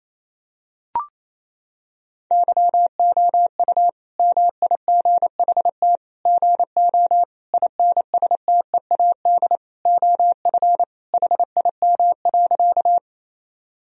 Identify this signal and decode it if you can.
Morse code, heard as a keyed tone, about 28 wpm, 700 Hz. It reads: YOU MIGHT GO INSTEAD OF HIM.
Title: Sentences from Top 1000 Words - Review Part 4 - 28wpm